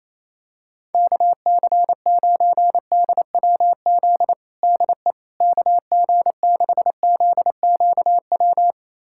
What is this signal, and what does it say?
Morse code, heard as a keyed tone, about 28 wpm, 700 Hz. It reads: KC9DWZ DE KG6ZQW
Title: USA Callsign DE Callsign - 6 Characters - 28wpm